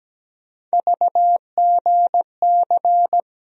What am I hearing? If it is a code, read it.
Morse code, heard as a keyed tone, about 17 wpm, 700 Hz. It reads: VGC
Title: USA Call Sign Postfixes - 3 Letters 17wpm